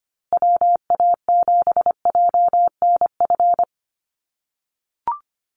Morse code, heard as a keyed tone, about 25 wpm, 700 Hz. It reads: WA7JNF
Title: USA Call Signs Repeated 3x 25wpm